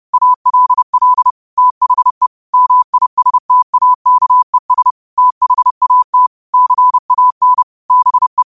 Morse code, heard as a keyed tone, about 30 wpm, 1000 Hz. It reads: ALL THE MISTAKES THAT CAN BE
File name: qotdfiles / 30wpm